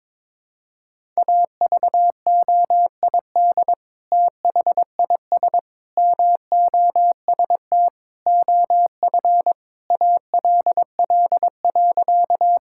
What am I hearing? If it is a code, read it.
Morse code, heard as a keyed tone, about 22 wpm, 700 Hz. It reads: AVOID THIS MOST OF ALL.
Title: Sentences from Top 1000 Words - Review Part 6 - 22wpm